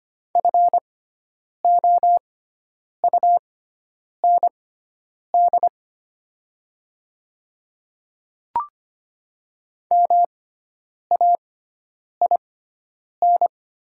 Morse code, heard as a keyed tone, about 25 wpm, spaced as for 9 wpm, 700 Hz. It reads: FOUND MAIN
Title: 6x Character Spacing - Top 300 Words - 25wpm